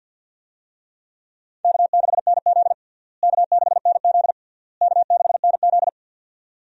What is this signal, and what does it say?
Morse code, heard as a keyed tone, about 52 wpm, 700 Hz. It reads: K6NZ K6NZ K6NZ
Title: USA Call Signs - 4 Characters - Repeated 3x - 50wpm